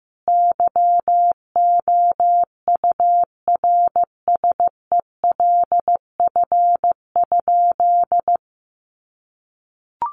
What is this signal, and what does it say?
Morse code, heard as a keyed tone, about 15 wpm, 700 Hz. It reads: YOURSELF?
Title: Sentences from Top 1500 Words 15wpm